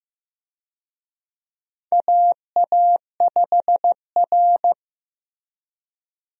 Morse code, heard as a keyed tone, about 16 wpm, 700 Hz. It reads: AA5R
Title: USA Call Signs - 4 Characters - Repeated 3x - 15wpm